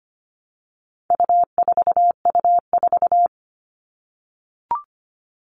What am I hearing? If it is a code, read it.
Morse code, heard as a keyed tone, about 25 wpm, 700 Hz. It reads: U4U4
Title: All Cut Numbers - 4-Digit 25wpm